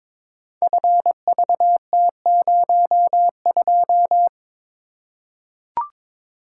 Morse code, heard as a keyed tone, about 22 wpm, 700 Hz. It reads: FVT02
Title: Code Groups - 5 Letter:Numbers - 22wpm